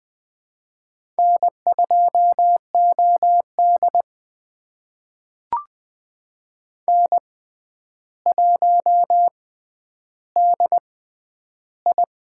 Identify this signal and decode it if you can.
Morse code, heard as a keyed tone, about 21 wpm, spaced as for 10 wpm, 700 Hz. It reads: N2OD N1DI
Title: USA Call Signs - 4 Characters - 6x Character Spacing - 20wpm